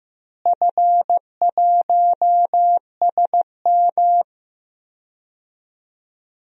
Morse code, heard as a keyed tone, about 15 wpm, 700 Hz. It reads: F1SM
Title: DXCC-100 Call Signs - 4 Characters - Repeated 2x - 15wpm